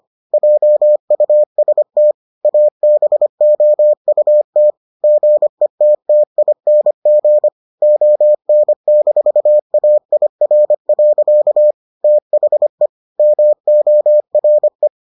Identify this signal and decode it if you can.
Morse code, heard as a keyed tone, about 25 wpm, 600 Hz. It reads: JUST ABOUT GETTING ON-AIR. THE MORE